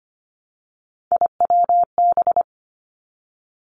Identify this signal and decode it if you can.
Morse code, heard as a keyed tone, about 25 wpm, 700 Hz. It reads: IWB
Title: Three Challenging Letter-Number Combinations 25wpm